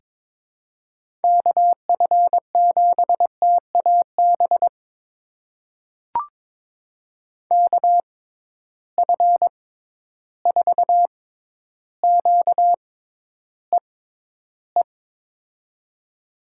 Morse code, heard as a keyed tone, about 22 wpm, spaced as for 11 wpm, 700 Hz. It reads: KF7TAB KF4QEE
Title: USA Call Signs - 6 Characters - 6x Character Spacing - 22wpm